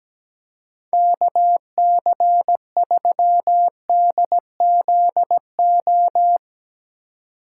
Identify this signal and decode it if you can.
Morse code, heard as a keyed tone, about 17 wpm, 700 Hz. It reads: KC3DZO
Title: USA Call Signs - 6 Characters - 2x Character Spacing - 17wpm